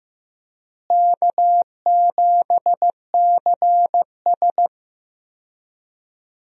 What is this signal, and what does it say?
Morse code, heard as a keyed tone, about 16 wpm, 700 Hz. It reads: K7CS
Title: USA Call Signs - 4 Characters - Repeated 2x - 15wpm